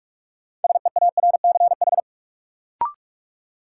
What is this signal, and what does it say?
Morse code, heard as a keyed tone, about 44 wpm, 700 Hz. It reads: SEARCH
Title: Sentences from Top 800 Words - New Word Drill - 45wpm